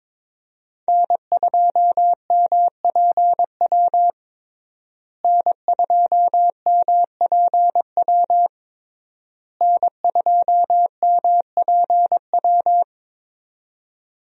Morse code, heard as a keyed tone, about 22 wpm, 700 Hz. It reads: N2MPW N2MPW N2MPW
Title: USA Call Signs - 5 Characters - Repeated 3x - 22wpm